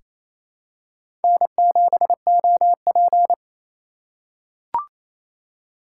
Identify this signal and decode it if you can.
Morse code, heard as a keyed tone, about 28 wpm, 700 Hz. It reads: N7OP
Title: USA Call Signs - 4 Characters - 3x Character Spacing - 28wpm